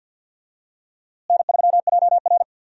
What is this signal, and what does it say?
Morse code, heard as a keyed tone, about 52 wpm, 700 Hz. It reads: N3JR
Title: USA Call Signs - 4 Characters - Repeated 3x - 50wpm